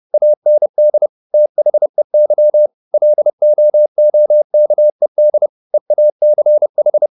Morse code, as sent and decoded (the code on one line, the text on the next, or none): .- -. -.. / - .... -.-- / .-.. --- --- -.- . -.. / . .- -.-. ....
AND THEY LOOKED EACH